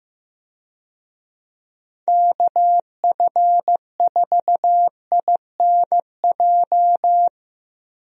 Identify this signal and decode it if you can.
Morse code, heard as a keyed tone, about 15 wpm, 700 Hz. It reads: KF4INJ
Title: USA Call Signs - 6 Characters - Repeated 3x - 15wpm